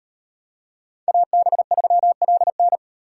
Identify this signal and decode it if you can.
Morse code, heard as a keyed tone, about 38 wpm, 700 Hz. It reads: AB3LN